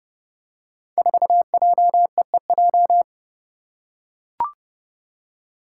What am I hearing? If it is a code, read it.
Morse code, heard as a keyed tone, about 30 wpm, 700 Hz. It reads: 4JEEJ
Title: Five Challenging Letter-Number Combinations 30wpm